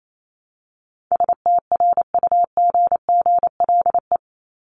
Morse code, heard as a keyed tone, about 28 wpm, 700 Hz. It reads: STRUGGLE
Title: Top 1751-2000 Words 28wpm